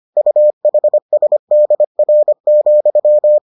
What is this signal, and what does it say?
Morse code, heard as a keyed tone, about 25 wpm, 600 Hz. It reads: UHSDR,